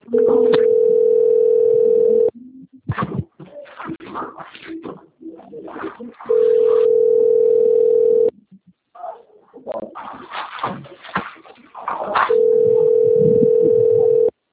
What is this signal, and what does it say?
Ringback tone (North America), 440 + 480 Hz.